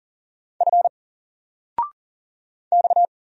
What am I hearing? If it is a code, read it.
Morse code, heard as a keyed tone, about 39 wpm, 700 Hz. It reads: F X
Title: Single Letter-Number 40wpm